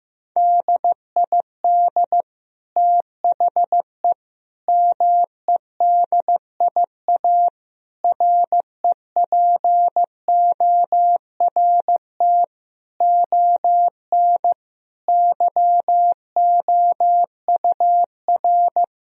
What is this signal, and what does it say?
Morse code, heard as a keyed tone, about 15 wpm, 700 Hz. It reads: DID THE MEDIA REPORT ON YOUR
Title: Sentences from Top 1000 Words - Review Part 4 - 15wpm